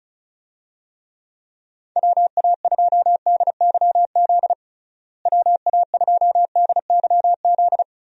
Morse code, heard as a keyed tone, about 36 wpm, 700 Hz. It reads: WA2DYZ WA2DYZ
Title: USA Call Signs - 6 Characters - Repeated 3x - 35wpm